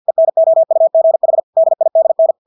Morse code, heard as a keyed tone, about 50 wpm, 650 Hz. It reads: ENOUGH DIDN
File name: Engelske ord 50 wpm